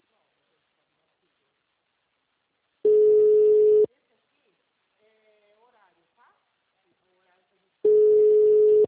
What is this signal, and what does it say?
Ringback tone (Europe, 425 Hz), with speech.
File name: out- -2043- - - .12285